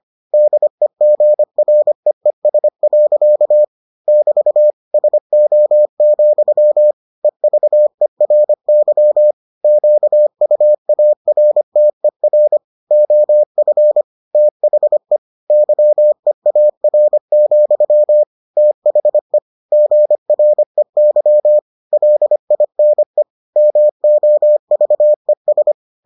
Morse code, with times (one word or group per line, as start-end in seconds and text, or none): 0.3-3.6 DEGREES.
4.1-4.7 <BT>
4.9-6.9 SO,
7.2-9.3 EVERY
9.6-12.6 QUARTER
12.9-14.0 OF
14.3-15.2 THE
15.5-18.2 YEAR,
18.6-19.4 THE
19.7-21.6 GREY
21.9-23.2 LINE
23.6-25.7 MOVES